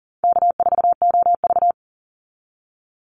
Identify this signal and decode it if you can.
Morse code, heard as a keyed tone, about 40 wpm, 700 Hz. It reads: K4OV
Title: Easy Call Signs Repeated 3x - 40wpm